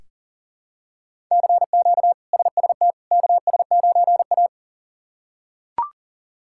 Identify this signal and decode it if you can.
Morse code, heard as a keyed tone, about 40 wpm, 700 Hz. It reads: CQ SST KS9A
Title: K1USN SST QSO Practice - 40wpm